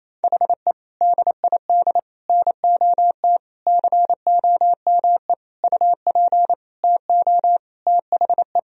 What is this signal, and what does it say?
Morse code, heard as a keyed tone, about 28 wpm, 700 Hz. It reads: HE DID NOT COME UP TO THE
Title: Sentences from Top 1000 Words - Review Part 3 - 28wpm